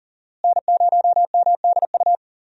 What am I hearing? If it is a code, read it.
Morse code, heard as a keyed tone, about 40 wpm, 700 Hz. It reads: N0MDU
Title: USA Call Signs - 5 Characters - Repeated 3x - 40wpm